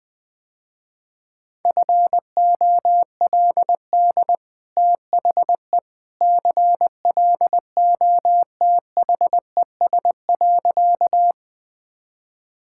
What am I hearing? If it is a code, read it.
Morse code, heard as a keyed tone, about 20 wpm, 700 Hz. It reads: FOLD THE CLOTHES.
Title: Easy Sentences with Hints - 20wpm